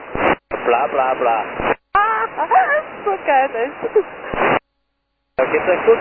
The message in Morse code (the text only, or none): none